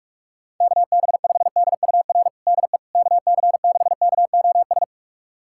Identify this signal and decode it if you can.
Morse code, heard as a keyed tone, about 46 wpm, 690 Hz. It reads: KB5DUR DE KC6KOS
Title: USA Callsign DE Callsign - 6 Characters - 45wpm